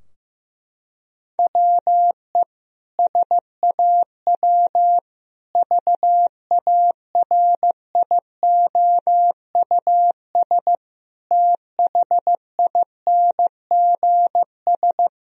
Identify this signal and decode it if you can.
Morse code, heard as a keyed tone, about 16 wpm, 700 Hz. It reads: WE SAW VARIOUS THINGS
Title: Sentences from Top 1000 Words - Review Part 4 - 15wpm